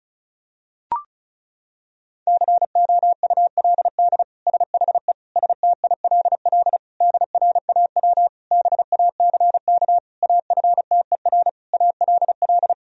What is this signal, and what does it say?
Morse code, heard as a keyed tone, about 34 wpm, 700 Hz. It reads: COULD SHE STILL DRAW BACK AFTER ALL
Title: Sentences from Top 1000 Words - Review Part 1 - 35wpm